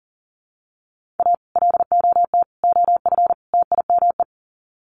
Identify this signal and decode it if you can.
Morse code, heard as a keyed tone, about 40 wpm, 700 Hz. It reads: A LOT OF TIME
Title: Sets of 4 Words 40wpm